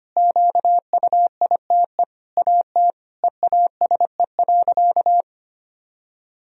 Morse code, heard as a keyed tone, about 25 wpm, 700 Hz. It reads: QUITE AT EASE.
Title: Sentences from Top 1000 Words - Review Part 1 - 25wpm